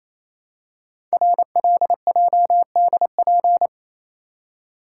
Morse code, heard as a keyed tone, about 28 wpm, 700 Hz. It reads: RLJDP
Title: Code Groups - 5 Letters - 28wpm